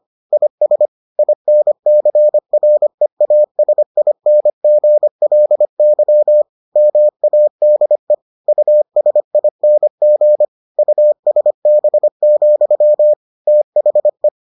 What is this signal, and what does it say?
Morse code, heard as a keyed tone, about 25 wpm, 600 Hz. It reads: IS INCREASINGLY MADE USING USB, THE